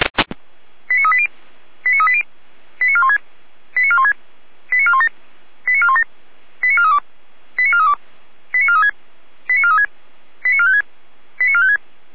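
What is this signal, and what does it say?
Five-tone selective call (ZVEI), heard as quick runs of five tones, tones of about 70 ms. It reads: 89380 89380 89416 89416 89417 89417 89432 89432 89436 89436 89456 89456
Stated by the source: ZVEI five-tone call received by SDR.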